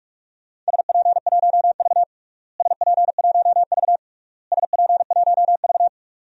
Morse code, heard as a keyed tone, about 46 wpm, 700 Hz. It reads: SP1V SP1V SP1V
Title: DXCC-100 Call Signs - 4 Characters - Repeated 3x - 45wpm